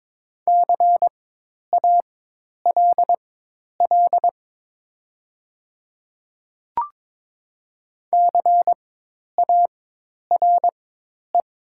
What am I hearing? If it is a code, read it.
Morse code, heard as a keyed tone, about 22 wpm, spaced as for 10 wpm, 700 Hz. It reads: CALL CARE
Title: 4x Character Spacing - Top 500 Words - 22wpm